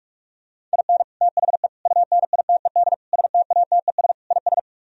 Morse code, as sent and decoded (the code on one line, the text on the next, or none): .. -. / - .... / ..- -. .. - . -.. / ... - .- - . ... / .. ...
IN THE UNITED STATES IS